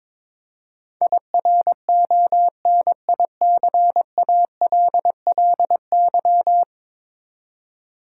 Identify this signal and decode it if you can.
Morse code, heard as a keyed tone, about 22 wpm, 700 Hz. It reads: IRONICALLY
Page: morse-code-ninja-practice-22wpm